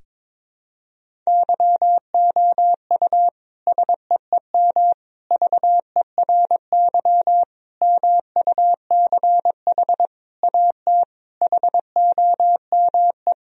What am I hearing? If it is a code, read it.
Morse code, heard as a keyed tone, about 22 wpm, 700 Hz. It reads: YOU SEEM VERY MUCH AT HOME
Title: Sentences from Top 1000 Words - Review Part 1 - 22wpm